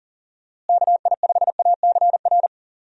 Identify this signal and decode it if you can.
Morse code, heard as a keyed tone, about 40 wpm, 700 Hz. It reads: KI5ACR
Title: USA Call Signs - 6 Characters - Repeated 2x - 40wpm